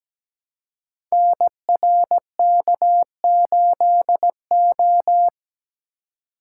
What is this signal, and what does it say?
Morse code, heard as a keyed tone, about 17 wpm, 700 Hz. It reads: NRK8O
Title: Code Groups - 5 Letter:Numbers - 17wpm